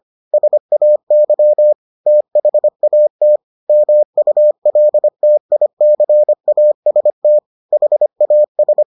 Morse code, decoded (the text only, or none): SAY THAT MULTICAST HAS